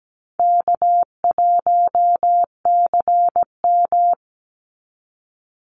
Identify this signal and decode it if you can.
Morse code, heard as a keyed tone, about 17 wpm, 700 Hz. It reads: K1CM
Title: NA SOTA Chaser Callsigns 17wpm